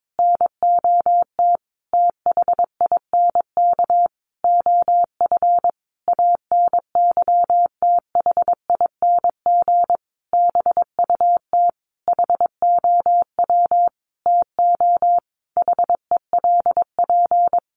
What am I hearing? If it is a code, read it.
Morse code, heard as a keyed tone, about 22 wpm, 700 Hz. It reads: NOT THINK OF ANYTHING BUT HOW TO HELP
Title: Sentences from Top 1000 Words 22wpm